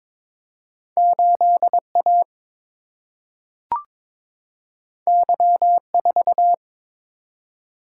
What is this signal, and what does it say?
Morse code, heard as a keyed tone, about 21 wpm, 700 Hz. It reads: 8 A Y 4
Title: Single-Letter Single-Number 22wpm